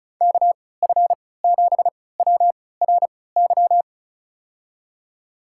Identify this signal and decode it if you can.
Morse code, heard as a keyed tone, about 36 wpm, spaced as for 20 wpm, 700 Hz. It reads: KF7WRY
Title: USA Call Signs - 6 Characters - 3x Character Spacing - 35wpm